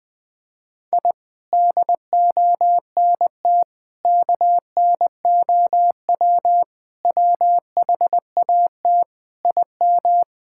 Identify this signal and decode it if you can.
Morse code, heard as a keyed tone, about 20 wpm, 700 Hz. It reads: I DONT KNOW WHAT IM
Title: Sets of 8 Words - 20wpm